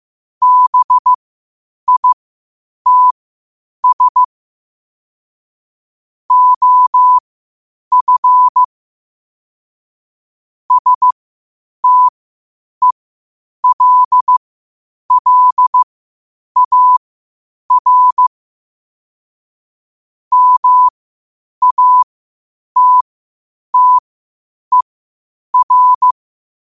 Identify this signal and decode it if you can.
Morse code, heard as a keyed tone, about 15 wpm, spaced as for 8 wpm, 1000 Hz. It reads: BITS OF STELLAR MATTER
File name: qotdfiles / 8wpm